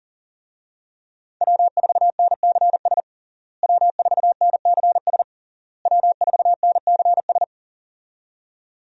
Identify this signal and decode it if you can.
Morse code, heard as a keyed tone, about 40 wpm, 700 Hz. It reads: W4NCS W4NCS W4NCS
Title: USA Call Signs - 5 Characters - Repeated 3x - 40wpm